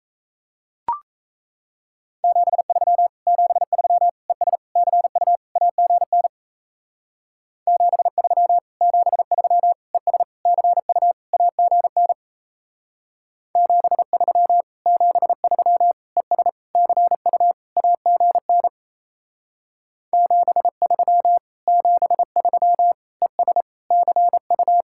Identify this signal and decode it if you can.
Morse code, heard as a keyed tone, about 36 wpm, 700 Hz. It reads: 73 73 ES CU AGN 73 73 ES CU AGN 73 73 ES CU AGN 73 73 ES CU